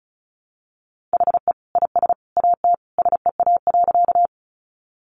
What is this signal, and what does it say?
Morse code, heard as a keyed tone, about 34 wpm, 700 Hz. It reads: HE IS AT SEA.
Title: Sentences from Top 300 Words 35wpm